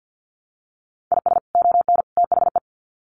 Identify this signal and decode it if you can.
Morse code, heard as a keyed tone, about 48 wpm, 700 Hz. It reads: IS ON THE
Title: Sets of 3 Words 50wpm